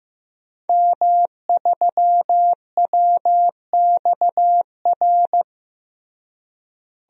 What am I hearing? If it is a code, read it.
Morse code, heard as a keyed tone, about 15 wpm, 700 Hz. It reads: M3WXR
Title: DXCC-100 Call Signs - 5 Characters - Repeated 3x - 15wpm